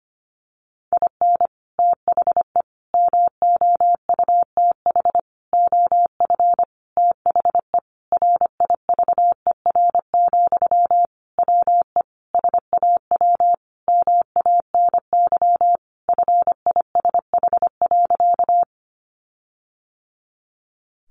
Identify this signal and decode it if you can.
Morse code, heard as a keyed tone, about 25 wpm, 700 Hz. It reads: IN THE MOUTH OF THE RIVER, WE SAW MANY FISH.
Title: Sentences from Top 1500 Words 25wpm